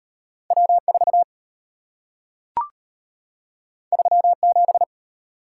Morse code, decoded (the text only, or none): W4 37